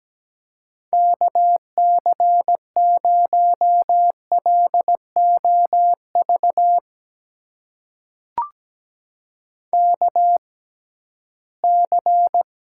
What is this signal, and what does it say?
Morse code, heard as a keyed tone, about 17 wpm, spaced as for 13 wpm, 700 Hz. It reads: KC0LOV KC
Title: USA Call Signs - 6 Characters - 6x Character Spacing - 17wpm